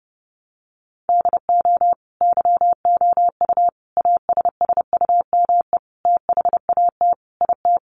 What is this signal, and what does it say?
Morse code, heard as a keyed tone, about 28 wpm, 700 Hz. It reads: DO YOU ASSUME THAT IT
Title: Sentences from Top 1500 Words 30wpm